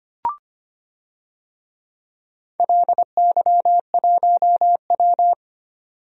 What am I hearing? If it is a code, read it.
Morse code, heard as a keyed tone, about 25 wpm, 700 Hz. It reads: LY1W
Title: DXCC-100 Call Signs - 4 Characters - 25wpm